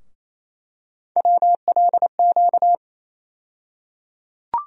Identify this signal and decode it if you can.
Morse code, heard as a keyed tone, about 28 wpm, 700 Hz. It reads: WLQ
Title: USA Call Sign Postfixes - 3 Letters 28wpm